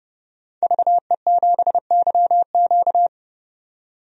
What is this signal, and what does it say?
Morse code, heard as a keyed tone, about 30 wpm, 700 Hz. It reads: VE7YQ
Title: DXCC-100 Call Signs - 5 Characters - 30wpm